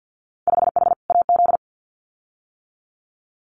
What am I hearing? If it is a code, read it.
Morse code, heard as a keyed tone, about 50 wpm, 700 Hz. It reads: 5H AZ
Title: Winter Field Day Exchanges Encore 50wpm